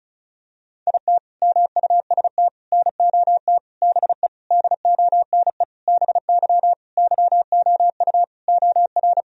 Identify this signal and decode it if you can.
Morse code, heard as a keyed tone, about 36 wpm, 700 Hz. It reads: IT MUST NOT BE DONE BY YOU OR
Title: Sentences from Top 1000 Words - Review Part 1 - 35wpm